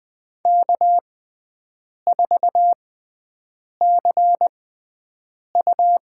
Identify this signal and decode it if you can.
Morse code, heard as a keyed tone, about 20 wpm, spaced as for 7 wpm, 700 Hz. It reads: K4CU
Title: USA Call Signs - 6x Character Spacing - 20wpm